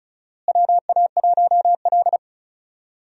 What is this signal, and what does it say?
Morse code, heard as a keyed tone, about 36 wpm, 700 Hz. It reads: WA1L
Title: USA Call Signs - 4 Characters - Repeated 3x - 35wpm